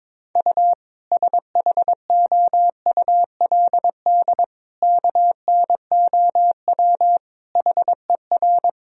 Morse code, heard as a keyed tone, about 22 wpm, 700 Hz. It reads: U SHOULD KNOW HER
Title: Sentences from Top 400 Words and Abbreviations 22wpm